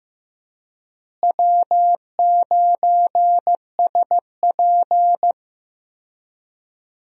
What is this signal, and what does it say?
Morse code, heard as a keyed tone, about 16 wpm, 700 Hz. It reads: W9SP
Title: USA Call Signs - 4 Characters - Repeated 3x - 15wpm